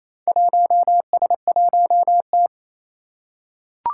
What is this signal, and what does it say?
Morse code, heard as a keyed tone, about 29 wpm, 700 Hz. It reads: 1S1T
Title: Four Challenging Letter-Number Combinations 28wpm